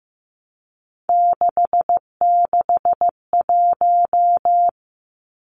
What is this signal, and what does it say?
Morse code, heard as a keyed tone, about 15 wpm, 700 Hz. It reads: 661
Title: Numbers - 3-Digit 15wpm